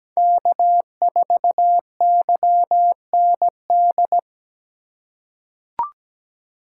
Morse code, heard as a keyed tone, about 17 wpm, 700 Hz. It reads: K4YND
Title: K1USN SST QSO Practice - 17wpm